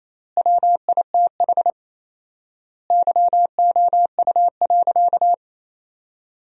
Morse code, heard as a keyed tone, about 28 wpm, 700 Hz. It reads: WITH YOU.
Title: 4x Word-Spacing - Sentences from Top 300 Words - 28wpm